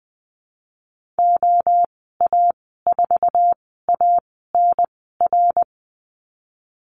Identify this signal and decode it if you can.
Morse code, heard as a keyed tone, about 20 wpm, spaced as for 15 wpm, 700 Hz. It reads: OA4ANR
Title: 055 Lesson 10g - Letter r - Call Signs 20wpm